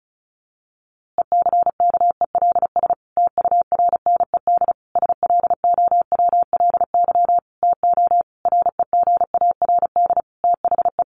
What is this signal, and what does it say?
Morse code, heard as a keyed tone, about 34 wpm, 700 Hz. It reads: ECKELS TURNED SLOWLY TO REGARD THE
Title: A Sound Of Thunder 35wpm